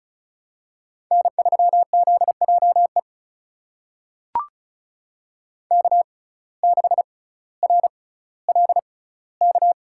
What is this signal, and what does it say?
Morse code, heard as a keyed tone, about 36 wpm, spaced as for 18 wpm, 700 Hz. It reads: N3ZJE K6RLK
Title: USA Call Signs - 5 Characters - 6x Character Spacing - 35wpm